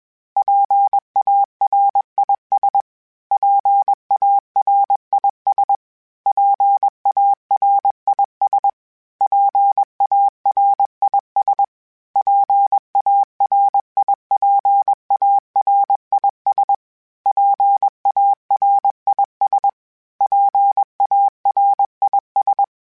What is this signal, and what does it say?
Morse code, heard as a keyed tone, about 21 wpm, 800 Hz. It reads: PARIS PARIS PARIS PARIS PARIPARIS PARIS PARIS
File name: PARIS Calibration / PARIS mp3s / PARIS@20